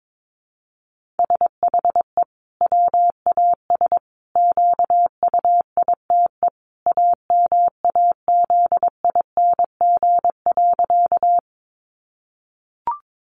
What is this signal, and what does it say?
Morse code, heard as a keyed tone, about 22 wpm, 700 Hz. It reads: SHE WAS QUITE AMAZING.
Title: Sentences from Top 1500 Words 22wpm